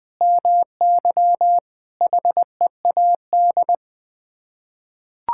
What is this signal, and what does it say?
Morse code, heard as a keyed tone, about 21 wpm, 700 Hz. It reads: MY HEAD
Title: Sets of 6 Words - 20wpm